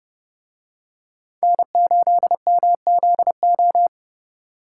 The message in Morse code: N8MZO

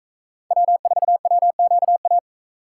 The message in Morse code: W4WQA

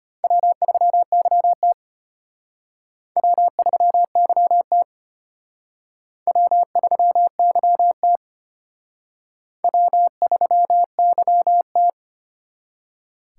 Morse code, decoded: W3YT W3YT W3YT W3YT

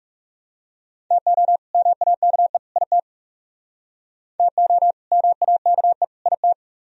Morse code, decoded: TO MAKE IT TO MAKE IT

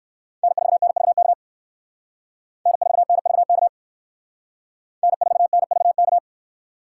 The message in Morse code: N4NVK N4NVK N4NVK

